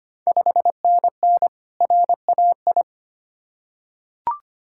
Morse code, heard as a keyed tone, about 25 wpm, 700 Hz. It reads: 5NN RAI
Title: Ontario QSO Party Exchange 25wpm